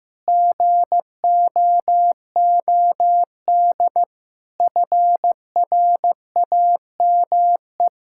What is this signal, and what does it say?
Morse code, heard as a keyed tone, about 16 wpm, 700 Hz. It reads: GOOD FRAME
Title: Sentences from Top 1000 Words - Review Part 6 - 15wpm